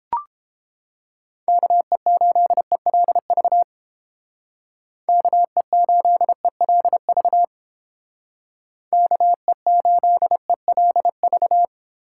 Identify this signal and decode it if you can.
Morse code, heard as a keyed tone, about 31 wpm, 700 Hz. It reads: KE8ELV KE8ELV KE8ELV